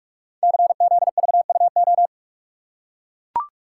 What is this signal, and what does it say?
Morse code, heard as a keyed tone, about 44 wpm, 700 Hz. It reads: CZVUO
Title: Code Groups - 5 Letter:Numbers - 45wpm